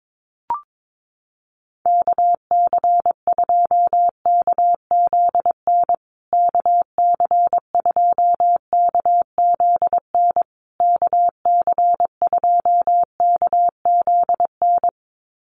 Morse code, heard as a keyed tone, about 22 wpm, 700 Hz. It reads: KC2KZN KC2KZN KC2KZN
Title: USA Call Signs Repeated 3x 22wpm